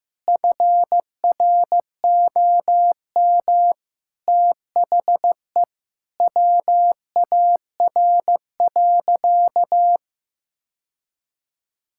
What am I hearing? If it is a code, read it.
Morse code, heard as a keyed tone, about 15 wpm, 700 Hz. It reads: FROM THE WAR.
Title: Sentences from Top 1000 Words - Review Part 6 - 15wpm